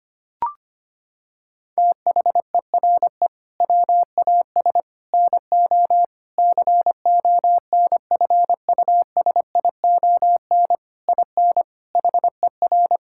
Morse code, heard as a keyed tone, about 25 wpm, 700 Hz. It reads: THERE WAS NO CONFUSION IN HER
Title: Sentences from Top 1600 Words 25wpm